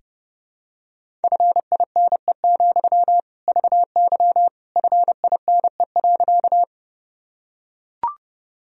Morse code, heard as a keyed tone, about 30 wpm, 700 Hz. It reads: FINE, VY FINE.
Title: Sentences from Top 500 Words and Abbreviations 30wpm